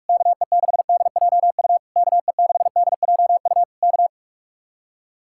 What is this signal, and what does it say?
Morse code, heard as a keyed tone, about 62 wpm, spaced as for 53 wpm, 700 Hz. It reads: KE6DJU KE6DJU K
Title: Speed-Racing Common QSO Phrases Encore - 45wpm